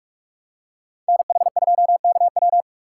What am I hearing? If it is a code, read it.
Morse code, heard as a keyed tone, about 42 wpm, 700 Hz. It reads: NH2KW
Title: USA Call Signs - 3x Character Spacing - 45wpm